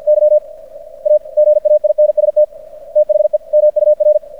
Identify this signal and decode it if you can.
Morse code, heard as a keyed tone, about 50 wpm, 600 Hz. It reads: Y AGAINST THE MUR
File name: archived / CRSnet